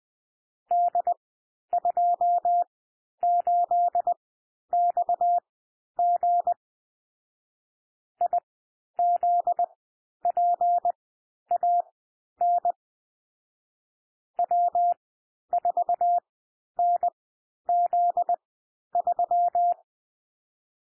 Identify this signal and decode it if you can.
Morse code, heard as a keyed tone, about 20 wpm, spaced as for 11 wpm, 700 Hz. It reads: D28XG IZPAN W4NZ3
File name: cwmp3 / M1 SOUND MIXED 20 wpm